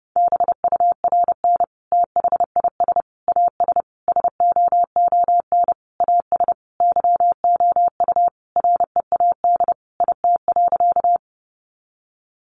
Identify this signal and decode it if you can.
Morse code, heard as a keyed tone, about 30 wpm, 700 Hz. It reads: BURN THIS AS SOON AS YOU READ IT.
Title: Sentences from Top 1300 Words 30wpm